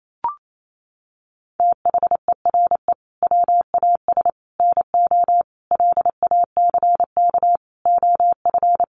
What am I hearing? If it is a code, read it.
Morse code, heard as a keyed tone, about 28 wpm, 700 Hz. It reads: THERE WAS NO LACK OF
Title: Sentences from Top 1100 Words 28wpm